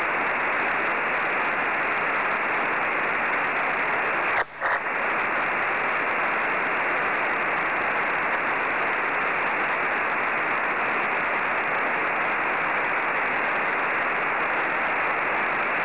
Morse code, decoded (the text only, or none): none